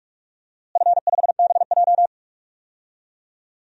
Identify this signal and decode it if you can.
Morse code, heard as a keyed tone, about 46 wpm, 700 Hz. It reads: F5BJ
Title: DXCC-100 Call Signs - 4 Characters - Repeated 2x - 45wpm